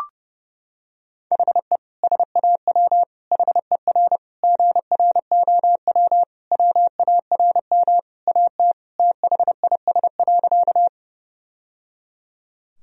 Morse code, heard as a keyed tone, about 30 wpm, 700 Hz. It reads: HE SAW HER GROW WARM AT THIS.
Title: Sentences from Top 1000 Words - Review Part 2 - 30wpm